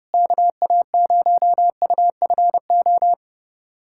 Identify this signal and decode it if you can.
Morse code, heard as a keyed tone, about 30 wpm, 700 Hz. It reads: KA0UFO
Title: USA Callsign DE Callsign 30wpm